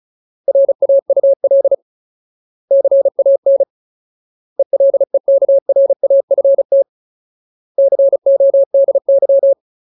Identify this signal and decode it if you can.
Morse code, heard as a keyed tone, about 34 wpm, 550 Hz. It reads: RAUL CAN ELEKRAFT CODY
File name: Ultimate Word List / 35WPM Ultimate